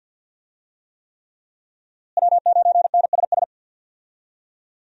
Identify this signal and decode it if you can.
Morse code, heard as a keyed tone, about 49 wpm, 700 Hz. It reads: W9NSS
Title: USA Call Signs - 5 Characters - Repeated 3x - 50wpm